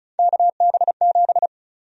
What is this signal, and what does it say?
Morse code, heard as a keyed tone, about 36 wpm, 700 Hz. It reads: KB7
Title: USA Call Sign Prefixes - 35wpm